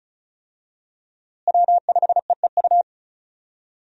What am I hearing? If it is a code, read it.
Morse code, heard as a keyed tone, about 34 wpm, 700 Hz. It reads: W5EEU
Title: USA Call Signs - 5 Characters - Repeated 2x - 35wpm